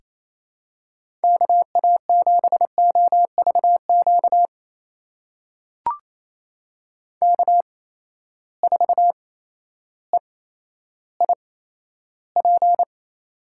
Morse code, heard as a keyed tone, about 28 wpm, spaced as for 13 wpm, 700 Hz. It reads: KA7OVQ K4EIP
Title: USA Call Signs - 8x Character Spacing - 28wpm